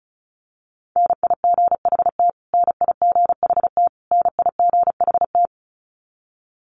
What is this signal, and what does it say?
Morse code, heard as a keyed tone, about 34 wpm, 700 Hz. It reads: NIGHT NIGHT NIGHT
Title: Top 500 Words Repeated 35wpm